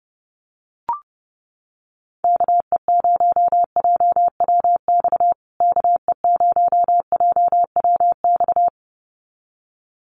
Morse code, heard as a keyed tone, about 30 wpm, 700 Hz. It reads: KE0JWX KE0JWX
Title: USA Call Signs Repeated 2x 30wpm